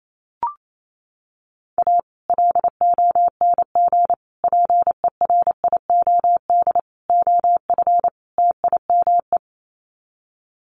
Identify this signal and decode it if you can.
Morse code, heard as a keyed tone, about 28 wpm, 700 Hz. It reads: A LONG PERIOD OF TIME
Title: Sets of 5 Words 28wpm